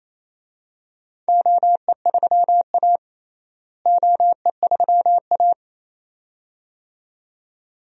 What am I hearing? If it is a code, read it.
Morse code, heard as a keyed tone, about 28 wpm, 700 Hz. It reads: OE3A OE3A
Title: DXCC-100 Call Signs - 4 Characters - Repeated 2x - 28wpm